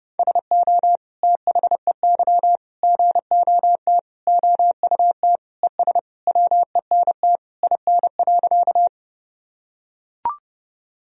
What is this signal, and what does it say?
Morse code, heard as a keyed tone, about 28 wpm, 700 Hz. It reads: SO THEY GOT OUT ES WENT IN.
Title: Sentences from Top 300 Words and Abbreviations 30wpm